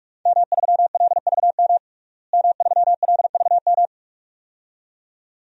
Morse code, heard as a keyed tone, about 46 wpm, 700 Hz. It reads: M3LVM M3LVM
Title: DXCC-100 Call Signs - 5 Characters - Repeated 2x - 45wpm